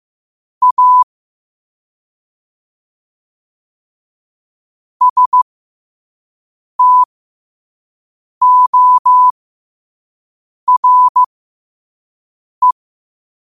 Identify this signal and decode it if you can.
Morse code, heard as a keyed tone, about 15 wpm, spaced as for 5 wpm, 1000 Hz. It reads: A STORE
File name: qotdfiles / 5wpm